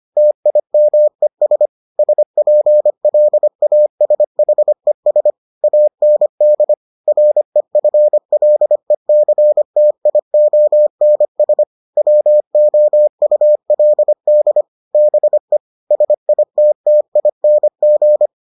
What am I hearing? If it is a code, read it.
Morse code, heard as a keyed tone, about 25 wpm, 600 Hz. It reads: TIMES SPLASHES AND REFLECTIONS WOULD BE SITTING